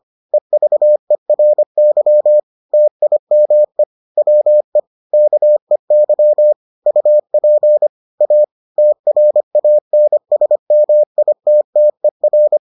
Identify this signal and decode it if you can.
Morse code, heard as a keyed tone, about 25 wpm, 600 Hz. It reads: EVERY TIME WE KEY UP A TRANSMITTER